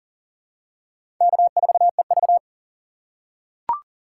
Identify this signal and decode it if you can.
Morse code, heard as a keyed tone, about 40 wpm, 700 Hz. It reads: K4EV
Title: USA Call Signs - 4 Characters - 3x Character Spacing - 40wpm